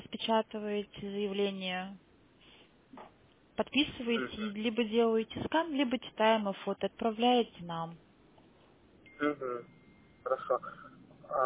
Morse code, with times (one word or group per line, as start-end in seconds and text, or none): none